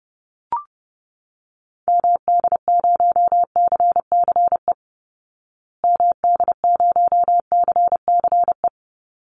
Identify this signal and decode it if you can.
Morse code, heard as a keyed tone, about 30 wpm, 700 Hz. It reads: MD0CCE MD0CCE
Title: Call Signs Repeated 2x - 30wpm